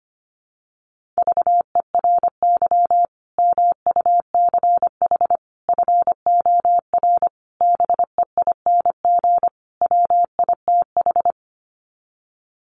Morse code, heard as a keyed tone, about 25 wpm, 700 Hz. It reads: VERY MUCH FOR BEING WITH
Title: Sets of 5 Words 25wpm